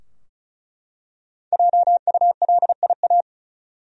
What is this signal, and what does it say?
Morse code, heard as a keyed tone, about 34 wpm, 700 Hz. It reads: JULIA
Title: USA Common Names - Max 5 Letters - 35wpm